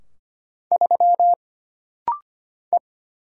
Morse code, heard as a keyed tone, about 25 wpm, 700 Hz. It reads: 3 E
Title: Single Letter-Number - Rapid-Fire 25wpm